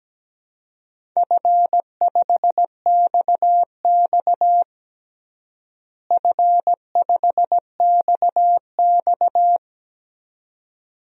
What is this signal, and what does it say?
Morse code, heard as a keyed tone, about 17 wpm, 700 Hz. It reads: F5XX F5XX
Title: DXCC-100 Call Signs - 4 Characters - Repeated 2x - 17wpm